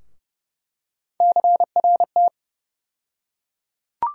Morse code, heard as a keyed tone, about 32 wpm, 700 Hz. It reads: CRT
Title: USA Call Sign Postfixes - 3 Letters 30wpm